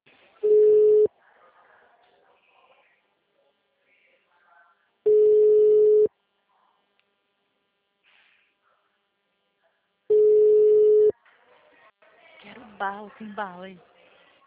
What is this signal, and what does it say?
Ringback tone (Europe, 425 Hz), with speech.